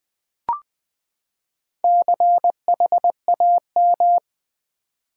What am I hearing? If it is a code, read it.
Morse code, heard as a keyed tone, about 20 wpm, 700 Hz. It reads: CHAM
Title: Build A Word - 4 Letters At A Time - 12-Letter Words - 20wpm